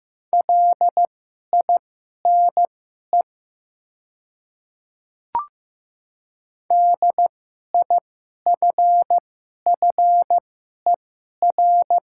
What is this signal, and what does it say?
Morse code, heard as a keyed tone, about 15 wpm, spaced as for 11 wpm, 700 Hz. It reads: LINE DIFFER
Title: 2x Character Spacing - Top 200 Words - 15wpm